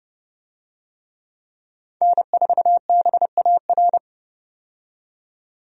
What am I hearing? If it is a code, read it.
Morse code, heard as a keyed tone, about 30 wpm, 700 Hz. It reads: N4BAR
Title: USA Call Signs - 5 Characters - 30wpm